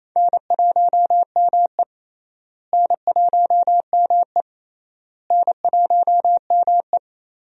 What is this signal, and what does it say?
Morse code, heard as a keyed tone, about 28 wpm, 700 Hz. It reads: N1ME N1ME N1ME
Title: USA Call Signs - 4 Characters - Repeated 3x - 28wpm